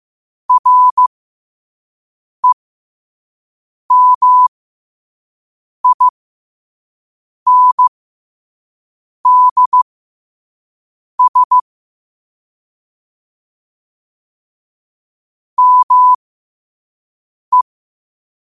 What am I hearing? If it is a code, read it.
Morse code, heard as a keyed tone, about 15 wpm, spaced as for 5 wpm, 1000 Hz. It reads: REMINDS ME
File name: qotdfiles / 5wpm